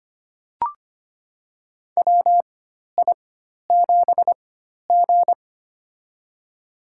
Morse code, heard as a keyed tone, about 25 wpm, spaced as for 12 wpm, 700 Hz. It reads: WI7G
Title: USA Call Signs - 4 Characters - 4x Character Spacing - 25wpm